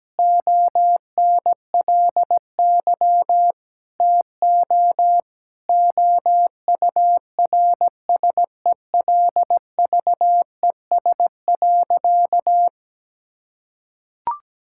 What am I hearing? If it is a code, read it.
Morse code, heard as a keyed tone, about 17 wpm, 700 Hz. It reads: ONLY TO OURSELVES.
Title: Sentences from Top 1600 Words 17wpm